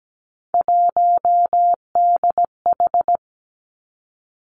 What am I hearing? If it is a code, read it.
Morse code, heard as a keyed tone, about 17 wpm, 700 Hz. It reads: 1DH
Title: Three Challenging Letter-Number Combinations 17wpm